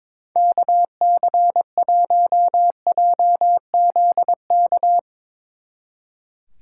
Morse code, heard as a keyed tone, about 22 wpm, 700 Hz. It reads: KC1JZK
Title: Speed Racing - USA Call Signs 22wpm